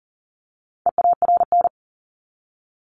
Morse code, heard as a keyed tone, about 40 wpm, 700 Hz. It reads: EARN
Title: Top 1251-1500 Words 40wpm